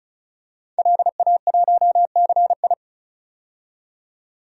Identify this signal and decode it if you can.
Morse code, heard as a keyed tone, about 36 wpm, 700 Hz. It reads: LA1CI
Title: DXCC-100 Call Signs - 5 Characters - 35wpm